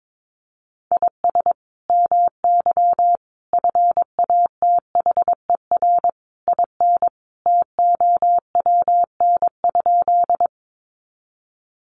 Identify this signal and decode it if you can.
Morse code, heard as a keyed tone, about 22 wpm, 700 Hz. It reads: IS MY FATHER IN TOWN?
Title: Sentences from Top 500 Words 22wpm